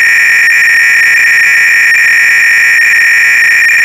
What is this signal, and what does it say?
300-baud modem data.
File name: Repository / ACR-Audio-Cassette-Programs / MITS-BASIC-GAME-BATTLE-CLOADC